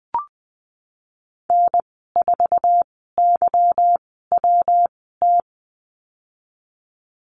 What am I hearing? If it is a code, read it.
Morse code, heard as a keyed tone, about 19 wpm, spaced as for 14 wpm, 700 Hz. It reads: N4YWT
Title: 161 Lesson 26g - Letter g - Call Signs 20wpm